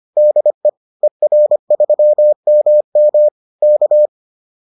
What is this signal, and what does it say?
Morse code, heard as a keyed tone, about 25 wpm, 600 Hz. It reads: DE ER3MM K
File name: 25wpm-6